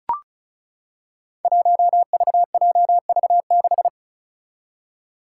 Morse code, heard as a keyed tone, about 36 wpm, 700 Hz. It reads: 1VJV6
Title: Five Challenging Letter-Number Combinations 35wpm